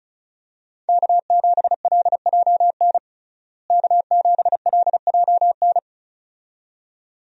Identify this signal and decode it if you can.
Morse code, heard as a keyed tone, about 36 wpm, 700 Hz. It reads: K7LJN K7LJN
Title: USA Call Signs - 5 Characters - Repeated 2x - 35wpm